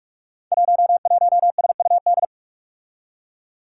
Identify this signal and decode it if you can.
Morse code, heard as a keyed tone, about 44 wpm, 700 Hz. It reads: 11SUD
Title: Five Challenging Letter-Number Combinations 45wpm